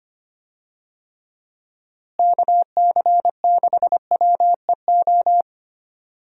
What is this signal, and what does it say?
Morse code, heard as a keyed tone, about 25 wpm, 700 Hz. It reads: KC6WEO
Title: USA Call Signs - 6 Characters - Repeated 2x - 25wpm